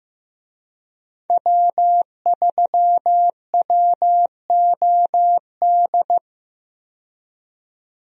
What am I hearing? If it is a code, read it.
Morse code, heard as a keyed tone, about 15 wpm, 700 Hz. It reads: W3WOD
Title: USA Call Signs - 5 Characters - 15wpm